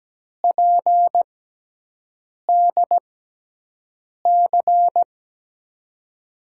Morse code, heard as a keyed tone, about 17 wpm, spaced as for 6 wpm, 700 Hz. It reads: PDC
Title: USA Call Signs - 6x Character Spacing - 17wpm